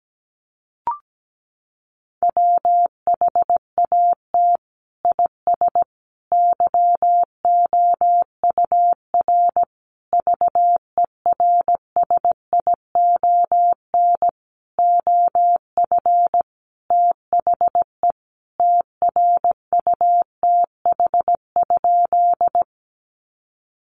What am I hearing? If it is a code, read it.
Morse code, heard as a keyed tone, about 17 wpm, 700 Hz. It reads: WHAT IS YOUR VERSION OF THE TRUTH?
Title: Sentences from Top 1400 Words 17wpm